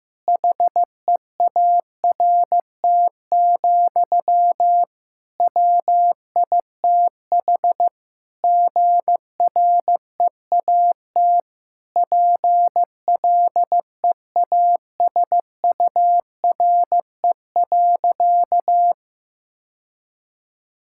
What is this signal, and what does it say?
Morse code, heard as a keyed tone, about 15 wpm, 700 Hz. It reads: HEART, WITH GREAT PLEASURE.
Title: Sentences from Top 1600 Words 15wpm